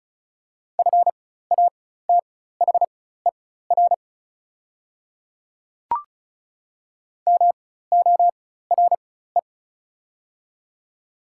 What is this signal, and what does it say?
Morse code, heard as a keyed tone, about 36 wpm, spaced as for 16 wpm, 700 Hz. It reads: FATHER MORE
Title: 4x Character Spacing - Top 200 Words - 35wpm